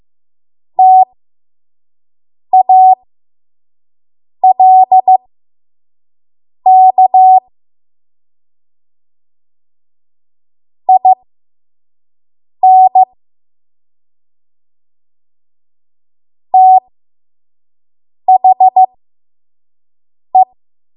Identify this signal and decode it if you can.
Morse code, heard as a keyed tone, about 15 wpm, spaced as for 5 wpm, 750 Hz. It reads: TALK IN THE
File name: Morse / 05WPM